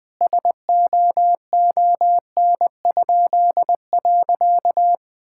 Morse code, heard as a keyed tone, about 20 wpm, 700 Hz. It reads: SOON?.
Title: Pangram Challenge v2 - 20wpm